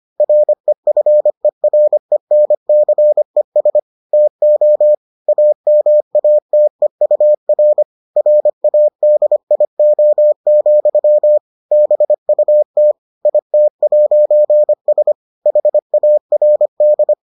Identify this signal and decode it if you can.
Morse code, heard as a keyed tone, about 25 wpm, 600 Hz. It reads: REFERENCES TO AMATEUR RADIO, BUT IT'S HARD